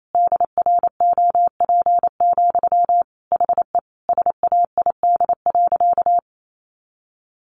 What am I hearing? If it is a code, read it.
Morse code, heard as a keyed tone, about 28 wpm, 700 Hz. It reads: DROP, HE SAID.
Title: Sentences from Top 1400 Words 28wpm